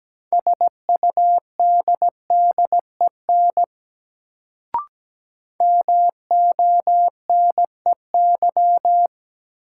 Morse code, heard as a keyed tone, about 17 wpm, 700 Hz. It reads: SUDDEN MONEY
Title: Sets of 4 Words - New Word Drill - 17wpm